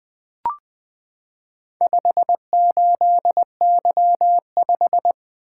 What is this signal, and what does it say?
Morse code, heard as a keyed tone, about 20 wpm, 700 Hz. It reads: 58Y5
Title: Four Letter-Number Combinations - 20wpm